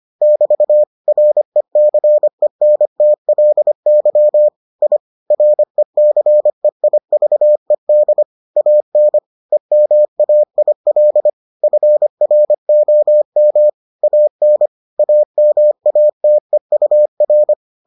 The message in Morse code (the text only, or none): <BT> RECENTLY I RECEIVED AN EMAIL FROM AN AMATEUR